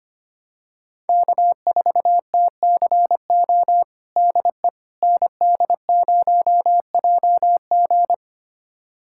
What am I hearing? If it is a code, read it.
Morse code, heard as a keyed tone, about 25 wpm, 700 Hz. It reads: K4TCO DE ND0JG
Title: USA Callsign DE Callsign - 5 Characters - 25wpm